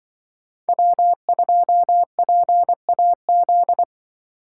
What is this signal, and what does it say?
Morse code, heard as a keyed tone, about 24 wpm, 700 Hz. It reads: W2PAZ
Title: Speed Racing - USA Call Signs 20wpm